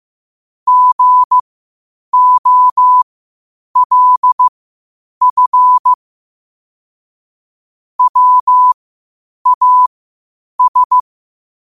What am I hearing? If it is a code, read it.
Morse code, heard as a keyed tone, about 15 wpm, spaced as for 8 wpm, 1000 Hz. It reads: GOLF WAS
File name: qotdfiles / 8wpm